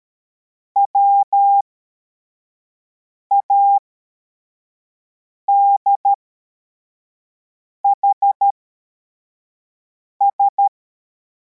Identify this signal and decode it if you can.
Morse code, heard as a keyed tone, about 13 wpm, spaced as for 4 wpm, 800 Hz. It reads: WADHS